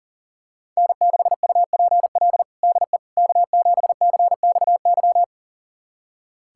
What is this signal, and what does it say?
Morse code, heard as a keyed tone, about 40 wpm, 700 Hz. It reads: N6UPL DE K7CXY
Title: USA Callsign DE Callsign - 5 Characters - 40wpm